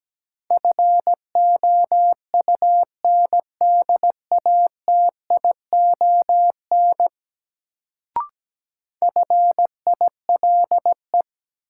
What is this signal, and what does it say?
Morse code, heard as a keyed tone, about 17 wpm, 700 Hz. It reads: FOUNDATION FILE
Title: Sentences from Top 1200 Words - New Word Drill Alt - 17wpm